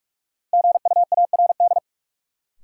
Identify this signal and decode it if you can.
Morse code, heard as a keyed tone, about 42 wpm, 700 Hz. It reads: GUARD
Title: Sentences from Top 700 Words - New Word Drill - 45wpm